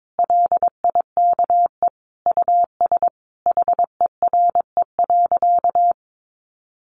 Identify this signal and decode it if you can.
Morse code, heard as a keyed tone, about 22 wpm, 700 Hz. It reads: LIKE US HERE.
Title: A Sound Of Thunder 22wpm